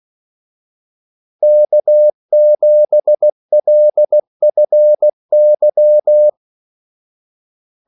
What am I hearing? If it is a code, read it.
Morse code, heard as a keyed tone, about 16 wpm, 600 Hz. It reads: K7LFY
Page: Kaldesignaler 16 wpm